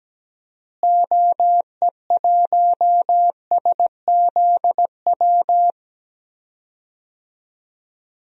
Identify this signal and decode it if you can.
Morse code, heard as a keyed tone, about 17 wpm, 700 Hz. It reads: OE1SZW
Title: DXCC-100 Call Signs - 6 Characters - Repeated 3x - 17wpm